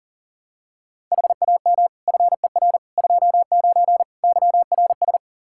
Morse code, heard as a keyed tone, about 40 wpm, 700 Hz. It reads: HAM FER 29 YRS
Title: Practice QSO - Exchange 3 - More Info - 40wpm